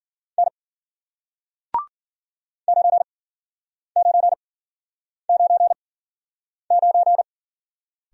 Morse code, heard as a keyed tone, about 54 wpm, spaced as for 9 wpm, 700 Hz. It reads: N 9999